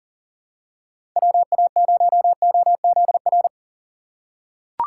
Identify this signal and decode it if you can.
Morse code, heard as a keyed tone, about 37 wpm, 700 Hz. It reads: WA0OZR
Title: USA Call Signs - 6 Characters - 2x Character Spacing - 40wpm